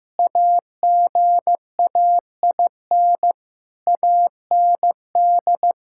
Morse code, heard as a keyed tone, about 15 wpm, 700 Hz. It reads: AGAIN AND
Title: Pangram Challenge v2 - 15wpm